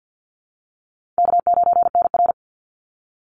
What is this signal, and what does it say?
Morse code, heard as a keyed tone, about 50 wpm, 700 Hz. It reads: K9NR
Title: Easy Call Signs - 50wpm